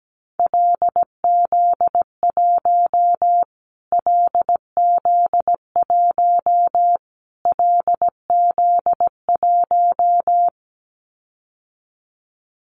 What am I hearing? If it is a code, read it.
Morse code, heard as a keyed tone, about 17 wpm, 700 Hz. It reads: LZ1 LZ1 LZ1
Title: Callsign Prefixes for Top 25 DXCC - Repeated 3x - 17wpm